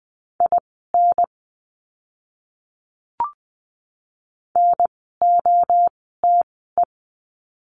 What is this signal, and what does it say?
Morse code, heard as a keyed tone, about 20 wpm, spaced as for 14 wpm, 700 Hz. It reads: IN NOTE
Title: 054 Lesson 10f - Letter r - Words 20wpm